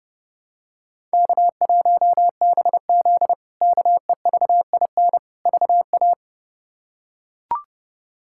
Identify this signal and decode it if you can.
Morse code, heard as a keyed tone, about 30 wpm, 700 Hz. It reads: K1BZ KEVIN VA
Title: K1USN SST QSO Practice - 30wpm